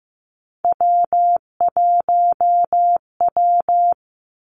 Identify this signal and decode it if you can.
Morse code, heard as a keyed tone, about 16 wpm, 700 Hz. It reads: W1W
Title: Three Challenging Letter-Number Combinations 15wpm